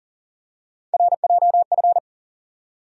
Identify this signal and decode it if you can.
Morse code, heard as a keyed tone, about 40 wpm, 700 Hz. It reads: RJF
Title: USA Call Sign Postfixes - 3 Letters 40wpm